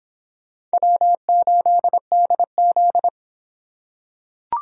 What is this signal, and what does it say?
Morse code, heard as a keyed tone, about 24 wpm, 700 Hz. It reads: W8DZ